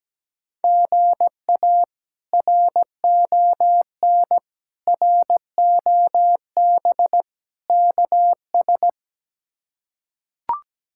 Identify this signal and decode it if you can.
Morse code, heard as a keyed tone, about 17 wpm, 700 Hz. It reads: GA RON ROB KS
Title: K1USN SST QSO Practice - 17wpm